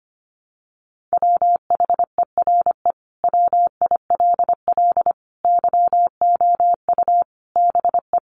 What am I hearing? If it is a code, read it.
Morse code, heard as a keyed tone, about 25 wpm, 700 Hz. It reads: WHERE WILL YOU BE
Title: Sentences from Top 900 Words 25wpm